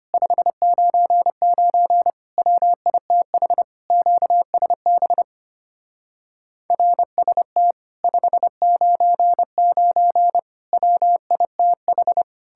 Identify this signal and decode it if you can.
Morse code, heard as a keyed tone, about 33 wpm, 700 Hz. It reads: 599 WITH QSB RST 599 WITH